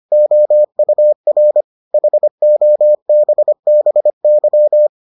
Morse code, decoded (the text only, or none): OUR HOBBY